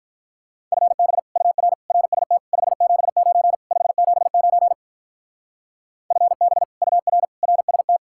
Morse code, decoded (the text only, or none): FB UR RST 579 579 FB UR RST